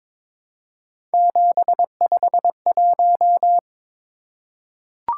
Morse code, heard as a keyed tone, about 22 wpm, 700 Hz. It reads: 751
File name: Three Letter-Number Combinations - 22wpm